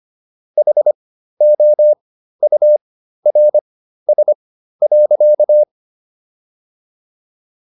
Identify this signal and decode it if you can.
Morse code, heard as a keyed tone, about 25 wpm, spaced as for 13 wpm, 600 Hz. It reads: HOURS.